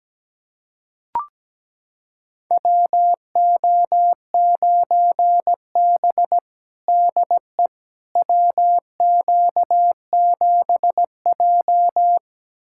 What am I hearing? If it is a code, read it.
Morse code, heard as a keyed tone, about 17 wpm, 700 Hz. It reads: WO9B DE WQ7J
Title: USA Callsign DE Callsign - 4 Characters - 17wpm